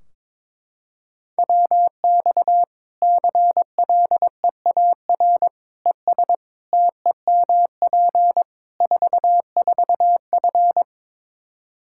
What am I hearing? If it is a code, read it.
Morse code, heard as a keyed tone, about 21 wpm, 700 Hz. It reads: WX CLEAR ES TEMP 44F
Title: Practice QSO - Exchange 2 - Info - 22wpm